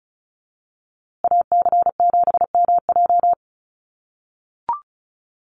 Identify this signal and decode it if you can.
Morse code, heard as a keyed tone, about 34 wpm, 700 Hz. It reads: AC7MJ
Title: USA Call Signs Repeated 2x 35wpm